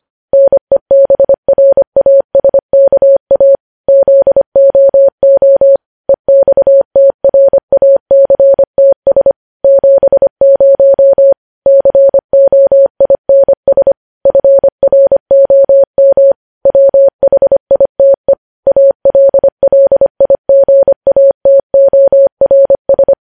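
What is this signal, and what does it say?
Morse code, heard as a keyed tone, about 25 wpm, 550 Hz. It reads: NEBRASKA ZOO EXTRACTS 70 COINS FROM WHITE ALLIGATORS
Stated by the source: Morse news headlines sent at 25 wpm.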